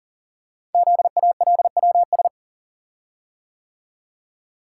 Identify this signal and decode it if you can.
Morse code, heard as a keyed tone, about 42 wpm, 700 Hz. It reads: ZALWS
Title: Code Groups - 5 Letters - 40wpm